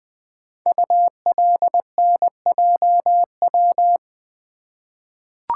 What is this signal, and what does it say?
Morse code, heard as a keyed tone, about 20 wpm, 700 Hz. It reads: ULNJW
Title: Code Groups - 5 Letters - 20wpm